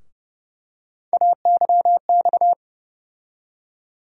Morse code, heard as a keyed tone, about 30 wpm, 700 Hz. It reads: AYX
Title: USA Call Sign Postfixes - 3 Letters 30wpm